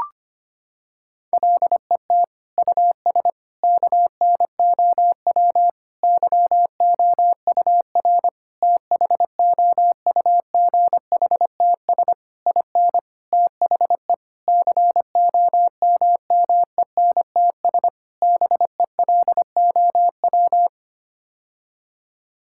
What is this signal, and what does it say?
Morse code, heard as a keyed tone, about 25 wpm, 700 Hz. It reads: LET US KNOW YOUR THOUGHTS IN THE COMMENTS BELOW
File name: Sets of 9 Words - 25wpm